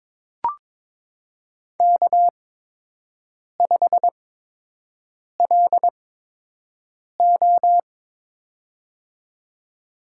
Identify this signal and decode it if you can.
Morse code, heard as a keyed tone, about 22 wpm, spaced as for 6 wpm, 700 Hz. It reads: K5LO
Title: USA Call Signs - 4 Characters - 8x Character Spacing - 22wpm